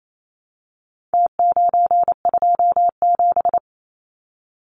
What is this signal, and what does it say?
Morse code, heard as a keyed tone, about 28 wpm, 700 Hz. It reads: T927
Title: Numbers - Contest Style - 4-Digit 28wpm